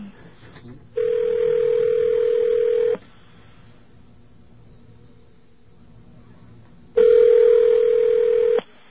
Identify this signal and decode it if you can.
Ringback tone (North America), 440 + 480 Hz.